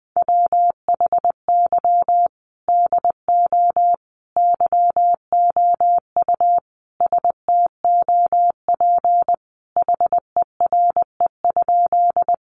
Morse code, heard as a keyed tone, about 20 wpm, 700 Hz. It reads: WHY DO YOU STOP HERE?
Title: Sentences from Top 300 Words 20wpm